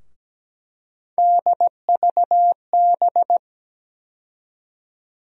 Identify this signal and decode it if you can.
Morse code, heard as a keyed tone, about 17 wpm, 700 Hz. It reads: DVB
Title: USA Call Sign Postfixes - 3 Letters 17wpm